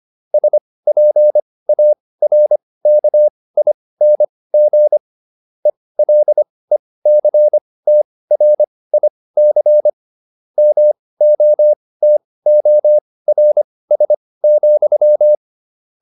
Morse code, heard as a keyed tone, about 25 wpm, spaced as for 18 wpm, 600 Hz. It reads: SPARKING ELECTRIC MOTORS,